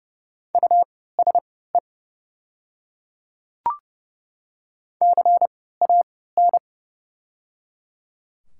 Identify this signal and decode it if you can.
Morse code, heard as a keyed tone, about 30 wpm, spaced as for 17 wpm, 700 Hz. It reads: USE CAN
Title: 3x Character Spacing - Top 200 Words - 30wpm